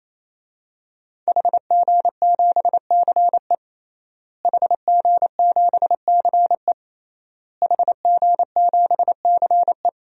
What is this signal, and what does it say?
Morse code, heard as a keyed tone, about 28 wpm, 700 Hz. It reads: HG7CE HG7CE HG7CE
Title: DXCC-100 Call Signs - 5 Characters - Repeated 3x - 28wpm